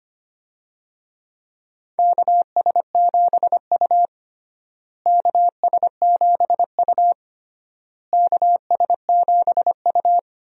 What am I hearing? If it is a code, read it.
Morse code, heard as a keyed tone, about 25 wpm, 700 Hz. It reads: KS7U KS7U KS7U
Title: USA Call Signs - 4 Characters - Repeated 3x - 25wpm